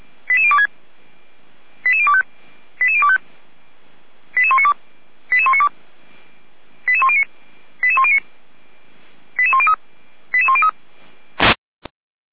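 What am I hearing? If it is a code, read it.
Five-tone selective call (ZVEI), heard as quick runs of five tones, tones of about 70 ms. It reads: 88026 88025 88025 88192 88192 88109 88109 88103 88103
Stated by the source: ZVEI five-tone call 88102 received by SDR, from audio.f85360.feuerwehr-orlamuende.org fire brigade server.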